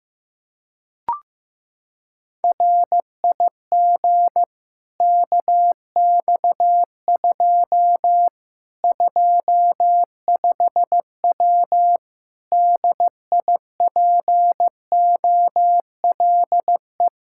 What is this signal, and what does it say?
Morse code, heard as a keyed tone, about 15 wpm, 700 Hz. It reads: RIG KX2 25W DIPOLE
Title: Practice QSO - Exchange 2 - Info - 15wpm